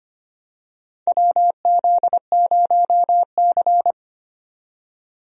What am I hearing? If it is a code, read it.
Morse code, heard as a keyed tone, about 25 wpm, 700 Hz. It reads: WZ0C
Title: USA Call Signs - 4 Characters - 25wpm